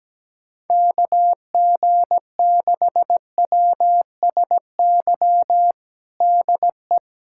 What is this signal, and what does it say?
Morse code, heard as a keyed tone, about 17 wpm, 700 Hz. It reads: KG6WSY DE
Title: USA Callsign DE Callsign - 6 Characters - 17wpm